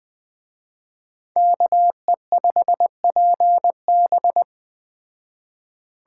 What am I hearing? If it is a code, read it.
Morse code, heard as a keyed tone, about 21 wpm, 700 Hz. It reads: KE5PB
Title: Speed Racing - USA 2x2 Call Signs 17wpm